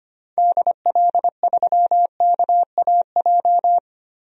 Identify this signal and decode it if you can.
Morse code, heard as a keyed tone, about 25 wpm, 700 Hz. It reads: DL3KAJ
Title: DXCC-100 Call Signs - 6 Characters - 25wpm